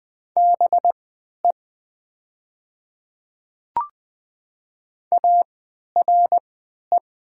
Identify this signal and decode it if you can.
Morse code, heard as a keyed tone, about 20 wpm, spaced as for 11 wpm, 700 Hz. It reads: BE ARE
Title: 3x Character Spacing - Top 100 Words - 20wpm